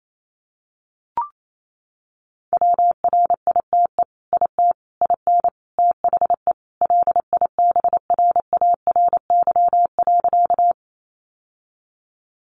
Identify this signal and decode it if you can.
Morse code, heard as a keyed tone, about 28 wpm, 700 Hz. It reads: WRITE IT IN THE LIBRARY.
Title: Sentences from Top 600 Words 28wpm